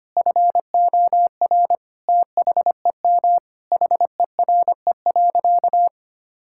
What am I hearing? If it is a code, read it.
Morse code, heard as a keyed tone, about 25 wpm, 700 Hz. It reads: FOR THEM HERE.
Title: Sentences from Top 1000 Words - Review Part 5 - 25wpm